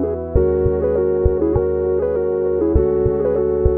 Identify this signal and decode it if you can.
There is no code.